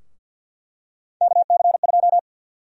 Morse code, heard as a keyed tone, about 49 wpm, 700 Hz. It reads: KC2
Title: USA Call Sign Prefixes - Repeated 2x - 50wpm